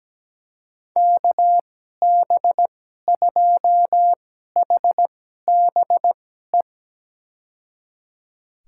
Morse code, heard as a keyed tone, about 17 wpm, spaced as for 12 wpm, 700 Hz. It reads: KB2HBE
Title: USA Call Signs - 6 Characters - 2x Character Spacing - 17wpm